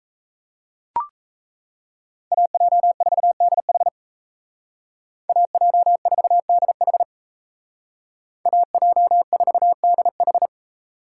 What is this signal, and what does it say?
Morse code, heard as a keyed tone, about 33 wpm, 700 Hz. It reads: AJ4DH AJ4DH AJ4DH
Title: Speed Racing - USA 2x2 Call Signs 28wpm